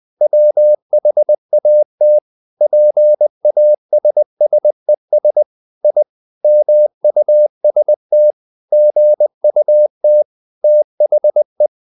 Morse code, heard as a keyed tone, about 20 wpm, 600 Hz. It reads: WHAT PASSES I MUST GUT THE
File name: the old man and the sea part 2 20wpm